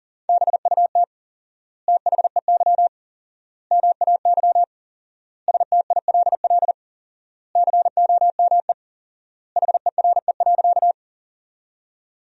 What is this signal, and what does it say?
Morse code, heard as a keyed tone, about 40 wpm, 700 Hz. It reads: BUT THEY MAY STILL COME HERE.
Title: 4x Word-Spacing - Sentences from Top 300 Words - 40wpm